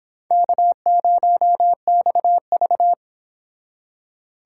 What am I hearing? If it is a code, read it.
Morse code, heard as a keyed tone, about 26 wpm, 700 Hz. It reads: K0XV